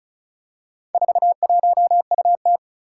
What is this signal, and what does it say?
Morse code, heard as a keyed tone, about 36 wpm, 700 Hz. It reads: 41UT
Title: Four Challenging Letter-Number Combinations 35wpm